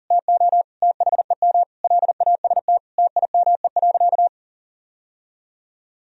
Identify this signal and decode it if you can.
Morse code, heard as a keyed tone, about 40 wpm, 700 Hz. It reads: TO THEM LAST TIME.
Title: Sentences from Top 1000 Words - Review Part 3 - 40wpm